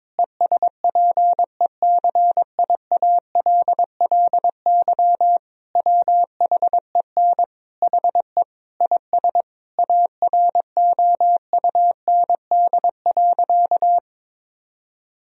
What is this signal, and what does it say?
Morse code, heard as a keyed tone, about 22 wpm, 700 Hz. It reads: ESPECIALLY WHEN HE IS AROUND.
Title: Sentences from Top 1000 Words - Review Part 4 - 22wpm